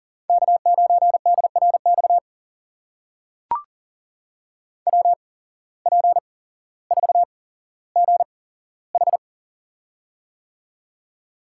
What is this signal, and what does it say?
Morse code, heard as a keyed tone, about 38 wpm, spaced as for 17 wpm, 700 Hz. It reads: K9DRX WP4GH
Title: USA Call Signs - 8x Character Spacing - 40wpm